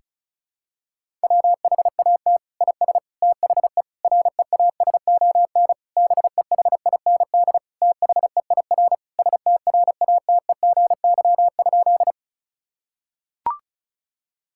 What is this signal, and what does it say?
Morse code, heard as a keyed tone, about 36 wpm, 700 Hz. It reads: WHAT IS THE REASON BEHIND THEIR STRATEGY?
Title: Sentences from Top 1000 Words - Review Part 5 - 35wpm